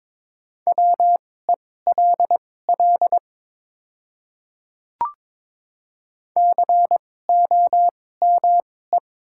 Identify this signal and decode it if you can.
Morse code, heard as a keyed tone, about 22 wpm, spaced as for 16 wpm, 700 Hz. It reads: WELL COME
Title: 2x Character Spacing - Top 300 Words - 22wpm